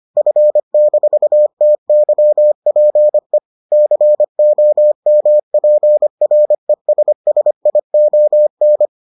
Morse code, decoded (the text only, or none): F-TYPE COMPRESSION